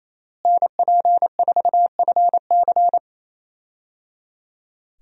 Morse code, heard as a keyed tone, about 28 wpm, 700 Hz. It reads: NP4FC
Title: USA Call Signs - 5 Characters - Repeated 3x - 28wpm